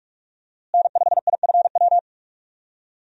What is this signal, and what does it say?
Morse code, heard as a keyed tone, about 46 wpm, 700 Hz. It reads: N5IFW
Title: USA Call Signs - 5 Characters - Repeated 2x - 45wpm